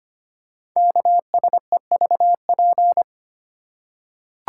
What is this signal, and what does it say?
Morse code, heard as a keyed tone, about 25 wpm, 700 Hz. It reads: KSEVP
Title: Code Groups - 5 Letter:Numbers - 25wpm